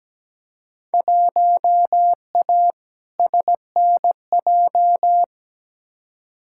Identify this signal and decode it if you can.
Morse code, heard as a keyed tone, about 17 wpm, 700 Hz. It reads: 1A SNJ
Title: Field Day Exchanges - 17wpm